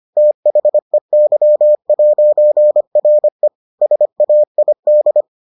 Morse code, heard as a keyed tone, about 25 wpm, 600 Hz. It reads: THEY'RE SAID